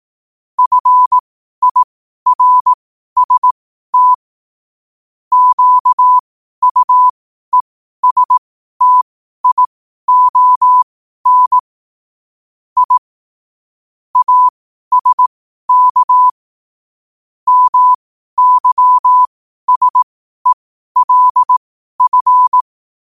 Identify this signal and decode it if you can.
Morse code, heard as a keyed tone, about 18 wpm, spaced as for 12 wpm, 1000 Hz. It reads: FIRST QUESTION I ASK MYSELF
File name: qotdfiles / 12wpm